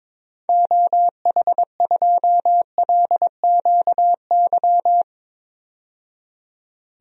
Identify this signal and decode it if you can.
Morse code, heard as a keyed tone, about 22 wpm, 700 Hz. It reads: OH2LQY
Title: DXCC-100 Call Signs - 6 Characters - 22wpm